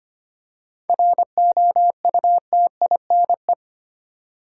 Morse code, heard as a keyed tone, about 25 wpm, 700 Hz. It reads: ROUTINE
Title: Seven Letter Words - 25wpm